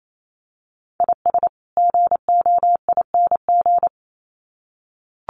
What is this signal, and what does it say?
Morse code, heard as a keyed tone, about 28 wpm, 700 Hz. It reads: IS GOING
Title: Sets of 2 Words 28wpm